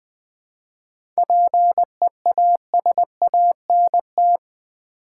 Morse code, heard as a keyed tone, about 20 wpm, 700 Hz. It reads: PEASANT
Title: 7 Letter Words - First 3 Letters Given - 20wpm